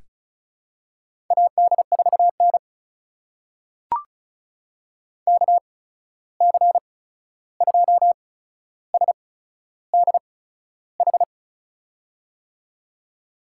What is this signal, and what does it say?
Morse code, heard as a keyed tone, about 36 wpm, spaced as for 13 wpm, 700 Hz. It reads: AD4N KC2SDH
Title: USA Call Signs - 8x Character Spacing - 35wpm